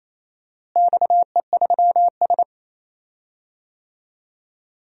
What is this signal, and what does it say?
Morse code, heard as a keyed tone, about 27 wpm, 700 Hz. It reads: XE3S
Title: DXCC-100 Call Signs - 4 Characters - Repeated 3x - 28wpm